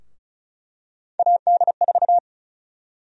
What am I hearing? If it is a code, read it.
Morse code, heard as a keyed tone, about 36 wpm, 700 Hz. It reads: AD4
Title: USA Call Sign Prefixes - Repeated 2x - 35wpm